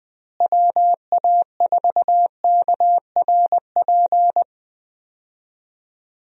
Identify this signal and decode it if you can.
Morse code, heard as a keyed tone, about 20 wpm, 700 Hz. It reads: WA4KRP
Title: USA Call Signs - 6 Characters - Repeated 3x - 20wpm